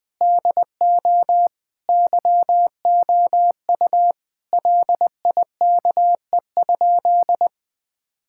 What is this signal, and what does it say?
Morse code, heard as a keyed tone, about 21 wpm, 700 Hz. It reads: DO YOU LIKE?
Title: Sentences with 3-Word Phrases - Spoken Beginning - 20wpm